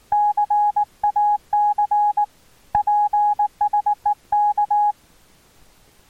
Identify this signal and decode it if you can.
Morse code, heard as a keyed tone, about 19 wpm, 830 Hz. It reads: CAC PSEK
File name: attachments / RX 28060 agc out res